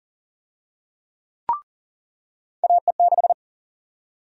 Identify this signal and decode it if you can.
Morse code, heard as a keyed tone, about 40 wpm, 700 Hz. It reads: AE6
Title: USA Call Sign Prefixes - 40wpm